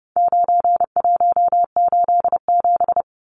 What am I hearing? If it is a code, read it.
Morse code, heard as a keyed tone, about 30 wpm, 700 Hz. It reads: 9187
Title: Numbers - 4-Digit 30wpm